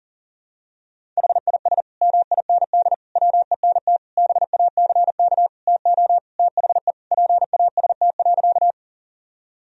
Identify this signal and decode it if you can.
Morse code, heard as a keyed tone, about 37 wpm, 700 Hz. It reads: HIS MIND WENT BACK TO THE PAST.
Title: Sentences from Top 1000 Words - Review Part 3 - 40wpm